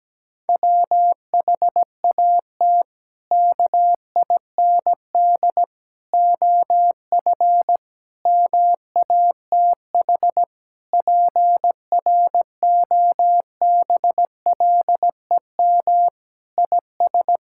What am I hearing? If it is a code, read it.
Morse code, heard as a keyed tone, about 17 wpm, 700 Hz. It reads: WHAT KIND OF MATH PROBLEM IS
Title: Sentences from Top 1000 Words - Review Part 6 - 17wpm